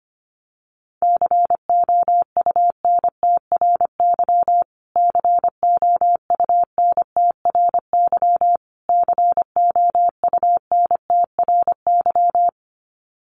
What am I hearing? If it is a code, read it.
Morse code, heard as a keyed tone, about 25 wpm, 700 Hz. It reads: COUNTRY COUNTRY COUNTRY
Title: Top 500 Words Repeated 25wpm